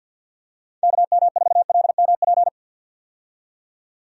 Morse code, heard as a keyed tone, about 46 wpm, 700 Hz. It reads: KM4LMP
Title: USA Call Signs - 6 Characters - 50wpm